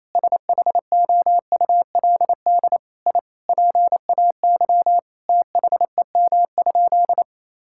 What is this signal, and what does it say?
Morse code, heard as a keyed tone, about 28 wpm, 700 Hz. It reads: SHOULD I PAY THEM?
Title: Sentences from Top 1000 Words - Review Part 5 - 28wpm